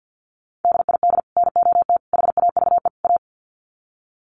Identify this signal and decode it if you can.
Morse code, heard as a keyed tone, about 52 wpm, 700 Hz. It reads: DID NOT HAVE A
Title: Sets of 4 Words 50wpm